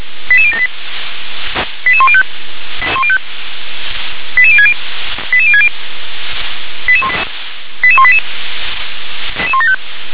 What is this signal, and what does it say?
Five-tone selective call (ZVEI), heard as quick runs of five tones, tones of about 70 ms. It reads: 88078 88105 88060 88060 88188
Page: /357/SDR/ZVEI/